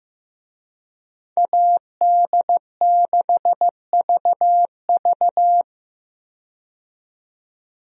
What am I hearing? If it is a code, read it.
Morse code, heard as a keyed tone, about 15 wpm, 700 Hz. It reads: AD6VV
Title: USA Call Signs - 5 Characters - 15wpm